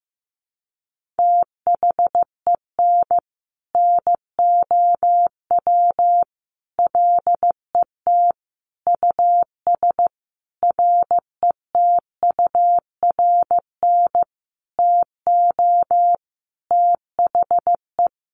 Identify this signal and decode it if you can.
Morse code, heard as a keyed tone, about 15 wpm, 700 Hz. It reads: THEN NOW LET US RETURN TO THE
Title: Sentences from Top 1500 Words 15wpm